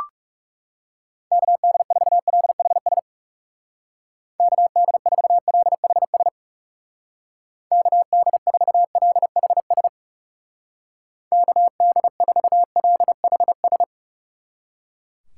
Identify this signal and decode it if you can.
Morse code, heard as a keyed tone, about 37 wpm, 700 Hz. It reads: KD4LHS KD4LHS KD4LHS KD4LHS
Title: Speed Racing - USA 2x3 Call Signs 30wpm